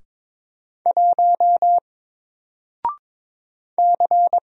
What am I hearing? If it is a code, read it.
Morse code, heard as a keyed tone, about 22 wpm, 700 Hz. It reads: 1 C
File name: Single-Letter-Number-22wpm